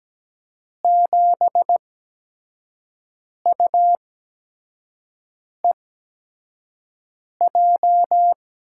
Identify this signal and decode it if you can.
Morse code, heard as a keyed tone, about 17 wpm, spaced as for 5 wpm, 700 Hz. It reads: 7UEJ
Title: USA Call Signs - 6 Characters - 8x Character Spacing - 17wpm